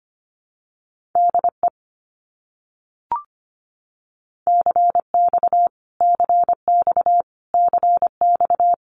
Morse code, heard as a keyed tone, about 25 wpm, 700 Hz. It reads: DE CX CX CX
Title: Abbreviations Repeated 3x - 25wpm